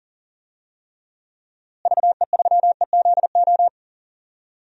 Morse code, heard as a keyed tone, about 40 wpm, 700 Hz. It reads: VE3EZO
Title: DXCC-100 Call Signs - 6 Characters - Repeated 3x - 40wpm